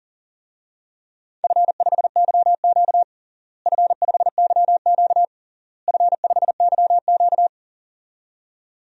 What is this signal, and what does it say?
Morse code, heard as a keyed tone, about 40 wpm, 700 Hz. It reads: F5YQ F5YQ F5YQ
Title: DXCC-100 Call Signs - 4 Characters - Repeated 3x - 40wpm